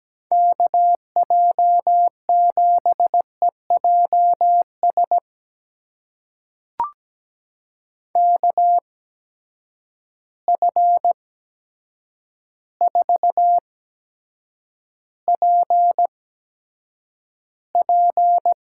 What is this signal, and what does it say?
Morse code, heard as a keyed tone, about 17 wpm, spaced as for 8 wpm, 700 Hz. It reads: KJ7EJS KF4PP
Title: USA Call Signs - 6 Characters - 8x Character Spacing - 17wpm